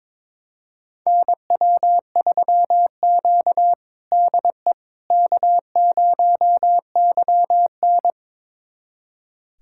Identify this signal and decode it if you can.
Morse code, heard as a keyed tone, about 22 wpm, 700 Hz. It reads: NW3Q DE K0YN
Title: USA Callsign DE Callsign - 4 Characters - 22wpm